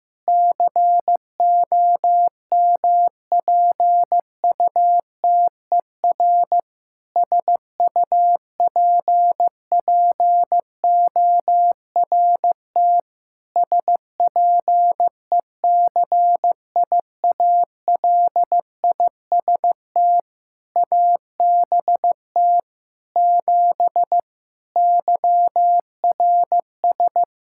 Morse code, heard as a keyed tone, about 15 wpm, 700 Hz. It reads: COMPUTER SUPPORT SPECIALIST ABT 7 YRS
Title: Practice QSO - Exchange 3 - More Info - 15wpm